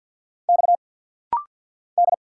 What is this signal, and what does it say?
Morse code, heard as a keyed tone, about 49 wpm, 700 Hz. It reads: X D
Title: Single Letter-Number Alt - Rapid-Fire 50wpm